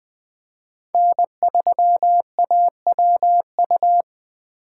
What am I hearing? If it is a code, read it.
Morse code, heard as a keyed tone, about 20 wpm, 700 Hz. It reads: N3AWU
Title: Speed Racing - USA 1x3 Call Signs 17wpm